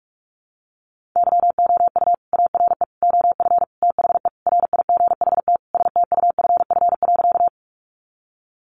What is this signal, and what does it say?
Morse code, heard as a keyed tone, about 44 wpm, 700 Hz. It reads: YOU ARE OF THE RIGHT STUFF.
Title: Sentences from Top 1400 Words 45wpm